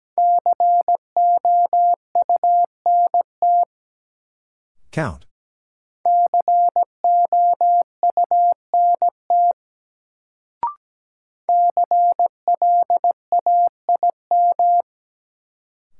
Morse code, heard as a keyed tone, about 17 wpm, 700 Hz. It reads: COUNT COUNT CLAIM
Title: Sentences from Top 1500 Words - New Word Drill - 17wpm